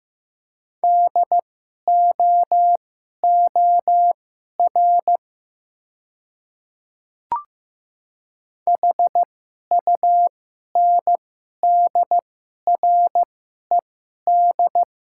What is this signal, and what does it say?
Morse code, heard as a keyed tone, about 15 wpm, spaced as for 11 wpm, 700 Hz. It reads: DOOR HUNDRED
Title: 2x Character Spacing - Top 500 Words - 15wpm